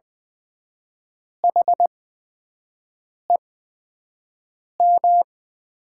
Morse code, heard as a keyed tone, about 20 wpm, spaced as for 5 wpm, 700 Hz. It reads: HEM
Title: 8x Character Spacing - Top 100 Words - 20wpm